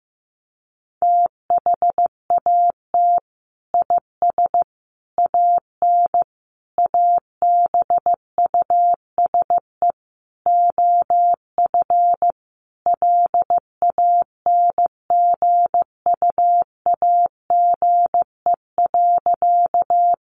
Morse code, heard as a keyed tone, about 15 wpm, 700 Hz. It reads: THAT IS AN ABUSE OF LANGUAGE.
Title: Sentences from Top 1300 Words 15wpm